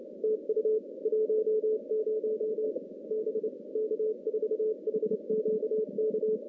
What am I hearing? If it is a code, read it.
Morse code, heard as a keyed tone, about 27 wpm, 410 Hz. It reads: TU 10 B K4HQK
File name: contests / CW-SS-k5tr- -2100